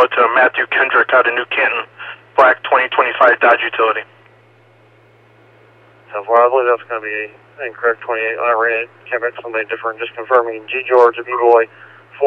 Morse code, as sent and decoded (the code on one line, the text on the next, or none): none